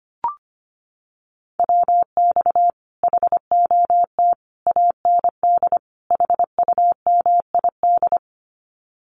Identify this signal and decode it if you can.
Morse code, heard as a keyed tone, about 25 wpm, 700 Hz. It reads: WX HOT AND HUMID
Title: Common QSO Phrases Encore 25wpm